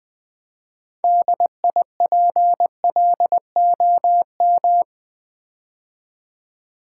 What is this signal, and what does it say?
Morse code, heard as a keyed tone, about 20 wpm, 700 Hz. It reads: DIPLOM
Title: Build A Word - 3 Letters At A Time - 8-Letter Words - 20wpm